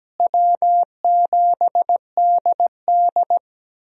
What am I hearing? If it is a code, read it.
Morse code, heard as a keyed tone, about 17 wpm, 700 Hz. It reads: W7DD
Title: USA Call Signs - 4 Characters - Repeated 3x - 17wpm